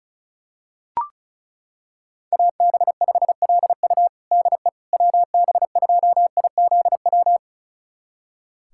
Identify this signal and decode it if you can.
Morse code, heard as a keyed tone, about 36 wpm, 700 Hz. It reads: AB5LU DE WB2IZW
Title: USA Callsign DE Callsign 35wpm